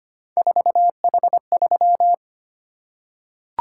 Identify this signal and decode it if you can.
Morse code, heard as a keyed tone, about 25 wpm, 700 Hz. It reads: 4H3
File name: Three Letter-Number Combinations - 25wpm